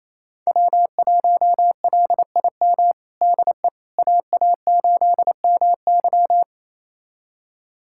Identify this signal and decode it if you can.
Morse code, heard as a keyed tone, about 28 wpm, 700 Hz. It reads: W1LIM DE AA8MY
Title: USA Callsign DE Callsign - 5 Characters - 28wpm